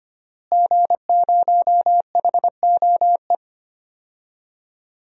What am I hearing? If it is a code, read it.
Morse code, heard as a keyed tone, about 25 wpm, 700 Hz. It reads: G0HOE
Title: DXCC-100 Call Signs - 5 Characters - Repeated 3x - 25wpm